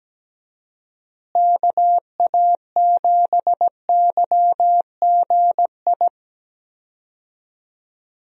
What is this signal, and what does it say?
Morse code, heard as a keyed tone, about 17 wpm, 700 Hz. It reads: KA7YGI
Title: USA Call Signs - 6 Characters - Repeated 3x - 17wpm